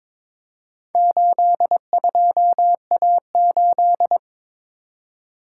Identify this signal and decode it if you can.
Morse code, heard as a keyed tone, about 22 wpm, 700 Hz. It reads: 82A8
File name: practice / 22wpm / Four Letter-Number Combinations - 22wpm